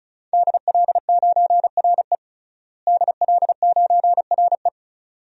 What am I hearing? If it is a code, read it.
Morse code, heard as a keyed tone, about 34 wpm, 700 Hz. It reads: DL9RE DL9RE
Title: DXCC-100 Call Signs - 5 Characters - Repeated 2x - 35wpm